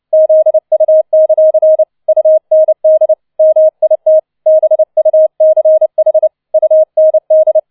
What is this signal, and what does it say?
Morse code, heard as a keyed tone, about 29 wpm, 630 Hz. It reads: ZU; UND MIT BUCH UND